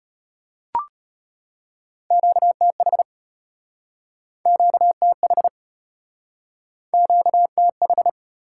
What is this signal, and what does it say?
Morse code, heard as a keyed tone, about 35 wpm, 700 Hz. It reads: QTH QTH QTH